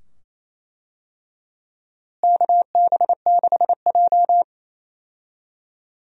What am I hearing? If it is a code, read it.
Morse code, heard as a keyed tone, about 27 wpm, 700 Hz. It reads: KB6J
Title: USA Call Signs - 4 Characters - Repeated 2x - 28wpm